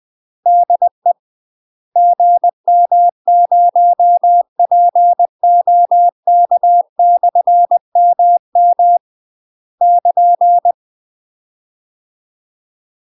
Morse code, heard as a keyed tone, about 20 wpm, 700 Hz. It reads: DE GM0POK/MM <KN>
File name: UK MASTER QSOs TRAINING / 20WPM / UK MASTER QSOs 0073